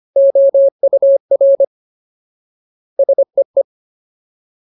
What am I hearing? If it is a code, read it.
Morse code, heard as a keyed tone, about 25 wpm, 550 Hz. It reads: OUR SEE
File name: Ultimate Word List / 25WPM Ultimate